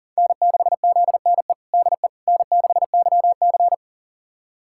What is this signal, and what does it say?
Morse code, heard as a keyed tone, about 40 wpm, 700 Hz. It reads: N6ZNE DE N6YC
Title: USA Callsign DE Callsign 40wpm